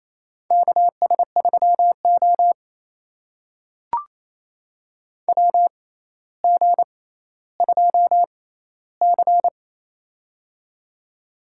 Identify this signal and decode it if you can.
Morse code, heard as a keyed tone, about 28 wpm, spaced as for 14 wpm, 700 Hz. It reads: KS3O WG2C
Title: USA Call Signs - 4 Characters - 6x Character Spacing - 28wpm